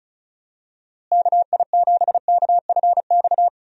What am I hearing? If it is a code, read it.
Morse code, heard as a keyed tone, about 34 wpm, 700 Hz. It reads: KI7KFX
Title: USA Call Signs - 6 Characters - Repeated 3x - 35wpm